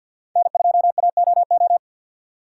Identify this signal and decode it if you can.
Morse code, heard as a keyed tone, about 46 wpm, 700 Hz. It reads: N2AOO
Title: USA Call Signs - 5 Characters - 4x Character Spacing - 50wpm